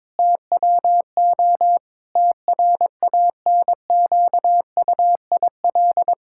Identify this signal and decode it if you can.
Morse code, heard as a keyed tone, about 22 wpm, 700 Hz. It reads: TWO TRANQUIL
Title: Pangram Challenge v2 - 22wpm